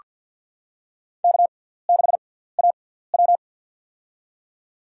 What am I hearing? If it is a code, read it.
Morse code, heard as a keyed tone, about 49 wpm, spaced as for 17 wpm, 700 Hz. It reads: K6AW
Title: USA Call Signs - 6x Character Spacing - 50wpm